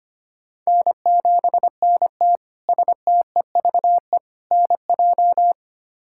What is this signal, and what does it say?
Morse code, heard as a keyed tone, about 25 wpm, 700 Hz. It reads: N7NT STEVE NJ
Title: K1USN SST QSO Practice - 25wpm